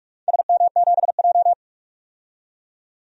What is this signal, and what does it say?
Morse code, heard as a keyed tone, about 46 wpm, 700 Hz. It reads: SM7J
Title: DXCC-100 Call Signs - 4 Characters - 45wpm